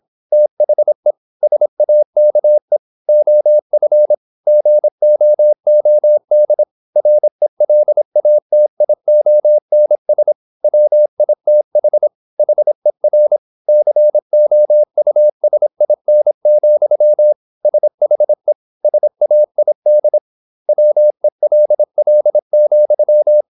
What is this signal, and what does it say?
Morse code, heard as a keyed tone, about 26 wpm, 600 Hz. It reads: THE SAKE OF GOOD RELATIONS WITH HER COUSIN, SHE SAID WELL,
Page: Never 26wpm | CW med Gnister